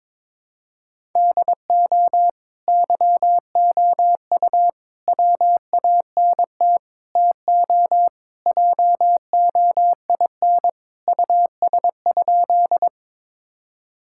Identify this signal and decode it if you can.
Morse code, heard as a keyed tone, about 22 wpm, 700 Hz. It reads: DO YOU WANT TO JOIN US?
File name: practice / 22wpm / Sentences with 4-Word Phrases - 22wpm